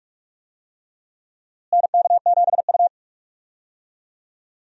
Morse code, heard as a keyed tone, about 46 wpm, 700 Hz. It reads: NK7U
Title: USA Call Signs - 4 Characters - Repeated 3x - 45wpm